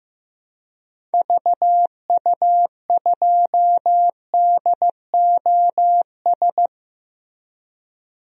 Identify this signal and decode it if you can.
Morse code, heard as a keyed tone, about 15 wpm, 700 Hz. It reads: VU2DOS
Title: DXCC-100 Call Signs - 6 Characters - Repeated 2x - 15wpm